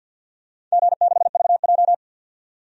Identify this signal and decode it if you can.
Morse code, heard as a keyed tone, about 52 wpm, 700 Hz. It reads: G6VJ
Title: DXCC-100 Call Signs - 4 Characters - 50wpm